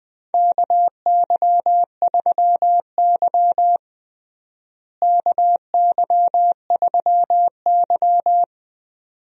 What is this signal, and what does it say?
Morse code, heard as a keyed tone, about 21 wpm, 700 Hz. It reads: KY3Y KY3Y
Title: USA Call Signs - 4 Characters - Repeated 2x - 20wpm